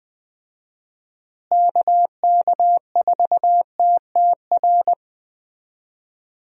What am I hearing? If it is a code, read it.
Morse code, heard as a keyed tone, about 20 wpm, 700 Hz. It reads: KK4TTR
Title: USA Call Signs - 6 Characters - Repeated 2x - 20wpm